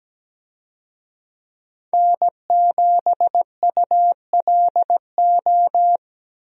Morse code, heard as a keyed tone, about 17 wpm, 700 Hz. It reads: N7ULO
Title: USA Call Signs - 5 Characters - 17wpm